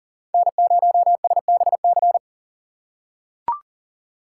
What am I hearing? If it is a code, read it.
Morse code, heard as a keyed tone, about 40 wpm, 700 Hz. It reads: N0SBC
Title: USA Call Signs - 5 Characters - 6x Character Spacing - 40wpm